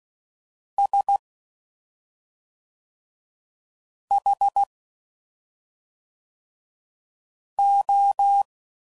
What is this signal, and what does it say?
Morse code, heard as a keyed tone, about 16 wpm, 800 Hz. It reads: S H O
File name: Suffixes / Special ING's / SlowToFaster ING's